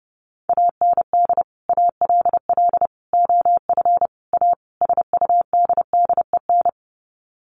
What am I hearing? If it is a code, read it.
Morse code, heard as a keyed tone, about 30 wpm, 700 Hz. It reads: AND ALL OF A SUDDEN
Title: Sets of 5 Words 30wpm